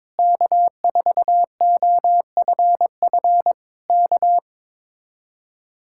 Morse code, heard as a keyed tone, about 22 wpm, 700 Hz. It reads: K4OFF K
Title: Speed-Racing Common QSO Phrases - 22wpm